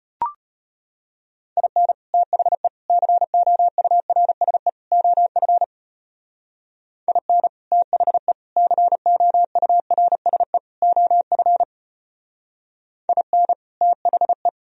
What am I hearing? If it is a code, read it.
Morse code, heard as a keyed tone, about 33 wpm, 700 Hz. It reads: IN THE COURSE OF IN THE COURSE OF IN THE